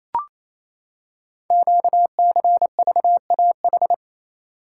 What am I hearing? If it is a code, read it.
Morse code, heard as a keyed tone, about 28 wpm, 700 Hz. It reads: QCVAH
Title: Code Groups - 5 Letters - 28wpm